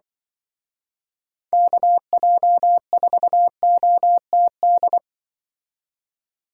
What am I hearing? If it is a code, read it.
Morse code, heard as a keyed tone, about 24 wpm, 700 Hz. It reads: KJ4OTD
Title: Speed Racing - USA Call Signs 20wpm